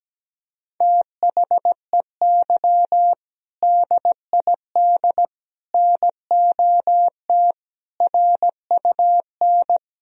Morse code, heard as a keyed tone, about 17 wpm, 700 Hz. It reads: THEY DID NOT RUN
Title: Sentences from Top 1000 Words - Review Part 1 - 17wpm